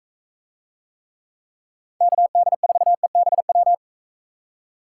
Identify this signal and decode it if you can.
Morse code, heard as a keyed tone, about 42 wpm, 700 Hz. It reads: KD4EBW